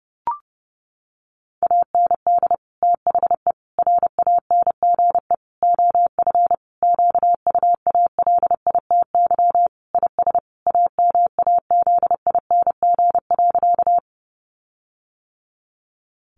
Morse code, heard as a keyed tone, about 30 wpm, 700 Hz. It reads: AND THE RANGE OF QUALITY IS AMAZING.
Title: Sentences from Top 1500 Words 30wpm